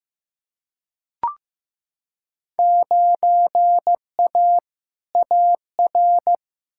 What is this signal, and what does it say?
Morse code, heard as a keyed tone, about 16 wpm, 700 Hz. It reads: 9A AR
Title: Field Day Exchanges Encore - 15wpm